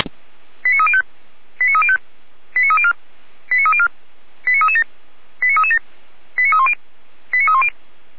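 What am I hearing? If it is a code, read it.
Five-tone selective call (ZVEI), heard as quick runs of five tones, tones of about 70 ms. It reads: 89305 89305 89304 89304 89337 89337 89310 89310
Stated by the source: ZVEI five-tone call 89276 received by SDR, on a server of Feuerwehr Orlamünde.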